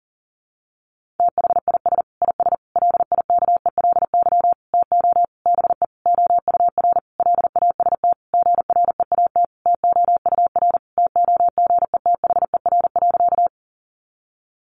Morse code, heard as a keyed tone, about 37 wpm, 700 Hz. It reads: THIS IS LIKELY TO BE OUR LAST GREAT TOUR TOGETHER.
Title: Sentences from Top 1400 Words 40wpm